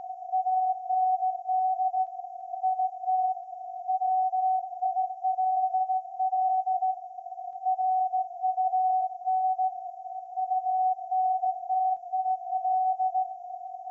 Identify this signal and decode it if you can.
Morse code, heard as a keyed tone, about 16 wpm, 750 Hz. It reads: AND IT WILL RUN UNTIL
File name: iz7slz / VLF / SAQ 30-06-2024 0900